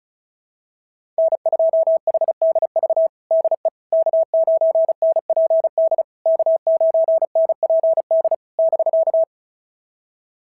Morse code, heard as a keyed tone, about 36 wpm, 650 Hz. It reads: N2HDV DE K9NPD K9NPD <BK>
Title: SOTA QSO Practice 35wpm